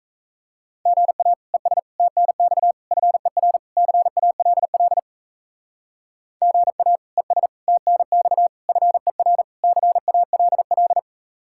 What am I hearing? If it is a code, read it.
Morse code, heard as a keyed tone, about 34 wpm, 700 Hz. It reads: GA ES TNX FER CALL GA ES TNX FER CALL
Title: Speed-Racing Common QSO Phrases - 28wpm